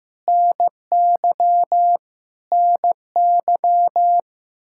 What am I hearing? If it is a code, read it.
Morse code, heard as a keyed tone, about 16 wpm, 700 Hz. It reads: NY NY
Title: Practice QSO - Exchange 1 - Beginning - 15wpm